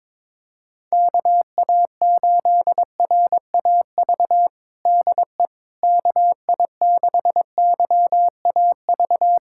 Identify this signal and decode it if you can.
Morse code, heard as a keyed tone, about 22 wpm, 700 Hz. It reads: KA8RAV DE KI6YAV
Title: USA Callsign DE Callsign - 6 Characters - 22wpm